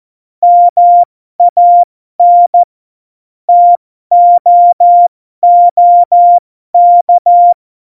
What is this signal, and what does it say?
Morse code, heard as a keyed tone, about 14 wpm, spaced as for 12 wpm, 700 Hz. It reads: MAN TOOK